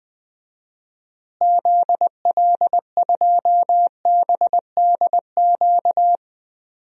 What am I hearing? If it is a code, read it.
Morse code, heard as a keyed tone, about 20 wpm, 700 Hz. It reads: ZL2BDQ
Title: DXCC-100 Call Signs - 6 Characters - Repeated 2x - 20wpm